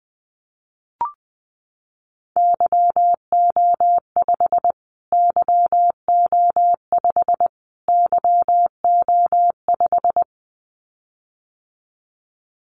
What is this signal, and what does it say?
Morse code, heard as a keyed tone, about 19 wpm, 700 Hz. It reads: YO5 YO5 YO5
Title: Callsign Prefixes for Top 25 DXCC - Repeated 3x - 20wpm